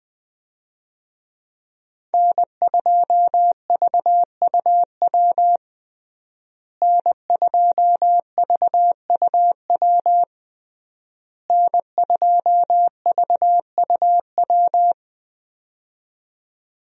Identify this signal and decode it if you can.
Morse code, heard as a keyed tone, about 21 wpm, 700 Hz. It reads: N2VUW N2VUW N2VUW
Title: USA Call Signs - 5 Characters - Repeated 3x - 20wpm